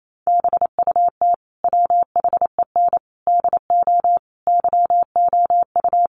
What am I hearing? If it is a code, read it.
Morse code, heard as a keyed tone, about 28 wpm, 700 Hz. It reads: BUT WHEN DO YOU
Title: Sentences from Top 1100 Words 28wpm